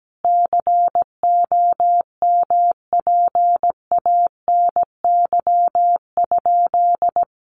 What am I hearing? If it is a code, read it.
Morse code, heard as a keyed tone, about 17 wpm, 700 Hz. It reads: COMPANY?
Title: Sentences from Top 1000 Words 17wpm